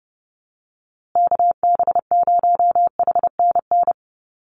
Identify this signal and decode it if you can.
Morse code, heard as a keyed tone, about 30 wpm, 700 Hz. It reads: KB0HNN
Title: USA Call Signs Repeated 2x 30wpm